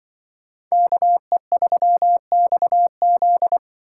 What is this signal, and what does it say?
Morse code, heard as a keyed tone, about 24 wpm, 700 Hz. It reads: KE3XZ
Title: Speed Racing - USA 2x2 Call Signs 20wpm